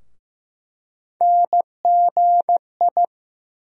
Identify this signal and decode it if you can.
Morse code, heard as a keyed tone, about 15 wpm, 700 Hz. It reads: NGI
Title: USA Call Sign Postfixes - 3 Letters 15wpm